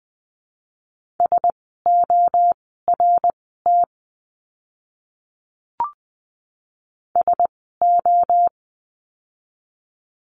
Morse code, heard as a keyed tone, about 20 wpm, spaced as for 14 wpm, 700 Hz. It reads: SORT SO
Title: 061 Lesson 11f - Letter h - Words 20wpm